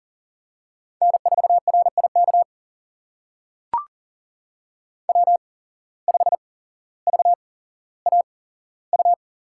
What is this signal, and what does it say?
Morse code, heard as a keyed tone, about 42 wpm, spaced as for 17 wpm, 700 Hz. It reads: N4RIK W5VAU
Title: USA Call Signs - 5 Characters - 8x Character Spacing - 40wpm